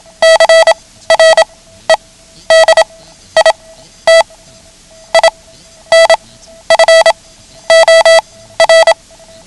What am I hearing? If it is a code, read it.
Morse code, heard as a keyed tone, about 27 wpm, spaced as for 13 wpm, 660 Hz. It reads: CREDIT INFOR
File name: ham radio / CodeQuick / Code Quick 09